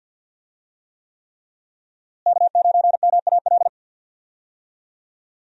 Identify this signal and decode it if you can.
Morse code, heard as a keyed tone, about 52 wpm, 700 Hz. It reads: K9MAL
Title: USA Call Signs - 5 Characters - Repeated 3x - 50wpm